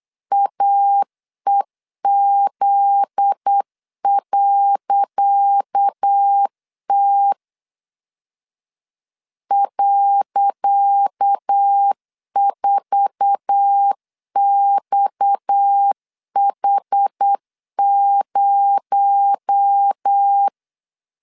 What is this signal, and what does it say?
Morse code, heard as a keyed tone, about 8 wpm, 800 Hz. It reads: AEZ.T .4XH0
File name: LEZIONI / 10 SIGNS / 10C DOT 8wpm 3649